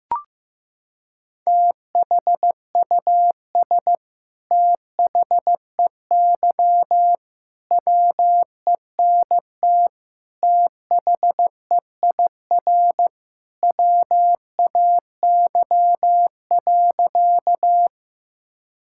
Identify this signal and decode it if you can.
Morse code, heard as a keyed tone, about 15 wpm, 700 Hz. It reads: THUS THEY WENT THEIR WAY.
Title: Sentences from Top 1000 Words - Review Part 5 - 15wpm